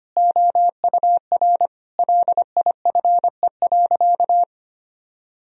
Morse code, heard as a keyed tone, about 25 wpm, 700 Hz. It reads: OUR LIFE.
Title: Sentences from Top 1000 Words - Review Part 1 - 25wpm